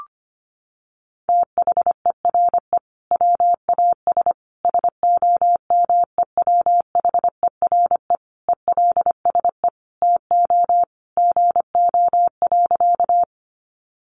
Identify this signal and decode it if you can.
Morse code, heard as a keyed tone, about 25 wpm, 700 Hz. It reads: THERE WAS SOMEWHERE ELSE TO GO.
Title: Sentences from Top 1400 Words 25wpm